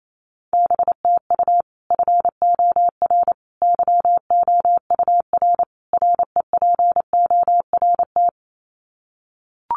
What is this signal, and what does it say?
Morse code, heard as a keyed tone, about 28 wpm, 700 Hz. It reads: BTU FOR YOUR REPORT
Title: Common QSO Phrases 28wpm